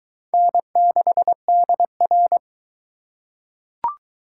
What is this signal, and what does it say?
Morse code, heard as a keyed tone, about 22 wpm, 700 Hz. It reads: N6DR